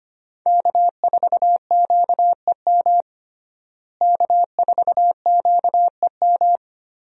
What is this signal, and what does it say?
Morse code, heard as a keyed tone, about 25 wpm, 700 Hz. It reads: K4QEM K4QEM
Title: USA Call Signs - 5 Characters - Repeated 2x - 25wpm